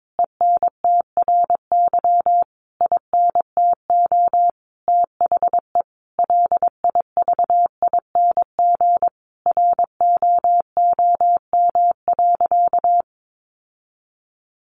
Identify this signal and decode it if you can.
Morse code, heard as a keyed tone, about 22 wpm, 700 Hz. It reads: ENTRY INTO THE LIVING ROOM.
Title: Sentences from Top 1500 Words 22wpm